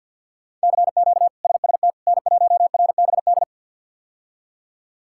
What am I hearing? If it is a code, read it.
Morse code, heard as a keyed tone, about 51 wpm, 700 Hz. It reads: CQ SST N1RBD
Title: K1USN SST QSO Practice - 50wpm